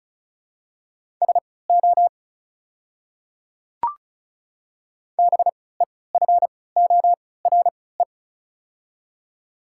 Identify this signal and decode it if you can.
Morse code, heard as a keyed tone, about 36 wpm, spaced as for 20 wpm, 700 Hz. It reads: SO BEFORE
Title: 3x Character Spacing - Top 200 Words - 35wpm